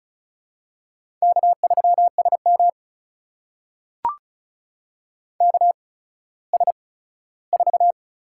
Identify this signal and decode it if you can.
Morse code, heard as a keyed tone, about 36 wpm, spaced as for 17 wpm, 700 Hz. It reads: K3SM KS4
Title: USA Call Signs - 4 Characters - 8x Character Spacing - 35wpm